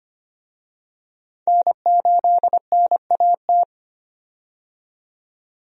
Morse code, heard as a keyed tone, about 25 wpm, 700 Hz. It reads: N8NAT
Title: USA Call Signs - 5 Characters - 25wpm